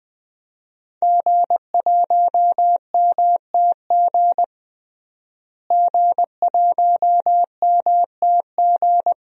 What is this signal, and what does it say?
Morse code, heard as a keyed tone, about 20 wpm, 700 Hz. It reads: G1MTG G1MTG
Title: DXCC-100 Call Signs - 5 Characters - Repeated 2x - 20wpm